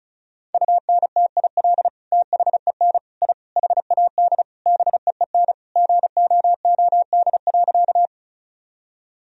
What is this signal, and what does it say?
Morse code, heard as a keyed tone, about 35 wpm, 700 Hz. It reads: UNTIL THEN I HAD BEEN GOOD.
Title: Sentences from Top 1000 Words - Review Part 1 - 35wpm